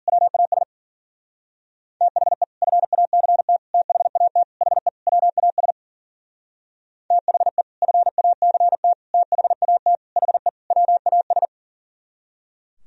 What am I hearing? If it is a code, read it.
Morse code, heard as a keyed tone, about 50 wpm, 700 Hz. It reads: WAS THE FACT THAT HE WAS THE FACT THAT HE WAS